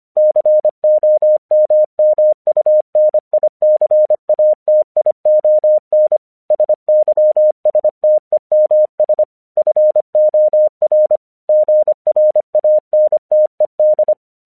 Morse code, heard as a keyed tone, about 25 wpm, 610 Hz. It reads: COMMUNICATION SYSTEMS FOR GRANTED
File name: 1 English 25wpm / AQC06